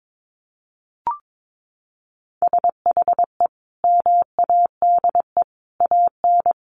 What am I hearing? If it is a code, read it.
Morse code, heard as a keyed tone, about 22 wpm, 700 Hz. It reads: SHE MADE AN
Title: Sentences from Top 1500 Words 22wpm